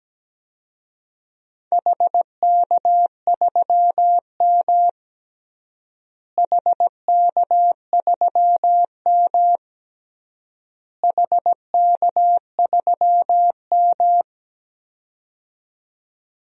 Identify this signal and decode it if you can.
Morse code, heard as a keyed tone, about 17 wpm, 700 Hz. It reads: HK3M HK3M HK3M
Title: DXCC-100 Call Signs - 4 Characters - Repeated 3x - 17wpm